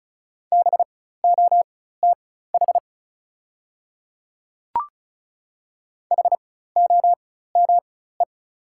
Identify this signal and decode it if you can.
Morse code, heard as a keyed tone, about 36 wpm, spaced as for 16 wpm, 690 Hz. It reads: BOTH HOME
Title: 4x Character Spacing - Top 500 Words - 35wpm